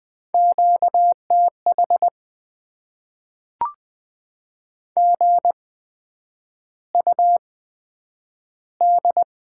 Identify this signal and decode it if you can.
Morse code, heard as a keyed tone, about 20 wpm, spaced as for 9 wpm, 700 Hz. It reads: QTH GUD
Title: Top 20 QSO Elements - 8x Character Spacing - 20wpm